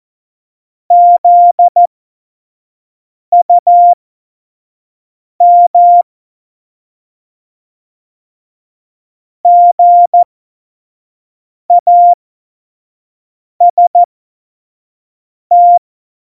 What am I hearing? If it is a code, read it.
Morse code, heard as a keyed tone, about 14 wpm, spaced as for 5 wpm, 700 Hz. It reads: ZUM GAST
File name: Morsekurs / Witze / Witz 22. Oktober 2019, 25 F70